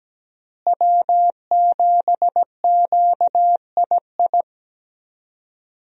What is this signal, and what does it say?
Morse code, heard as a keyed tone, about 17 wpm, 700 Hz. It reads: W7QII
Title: USA Call Signs - 5 Characters - Repeated 2x - 17wpm